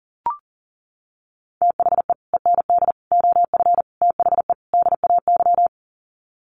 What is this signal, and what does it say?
Morse code, heard as a keyed tone, about 40 wpm, 700 Hz. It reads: THE END OF THE DAY
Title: Sets of 5 Words 40wpm